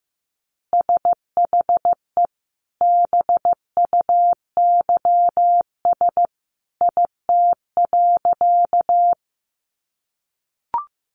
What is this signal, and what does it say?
Morse code, heard as a keyed tone, about 14 wpm, 700 Hz. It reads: SHE BUYS IT.
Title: Additional Easy Phrases 15wpm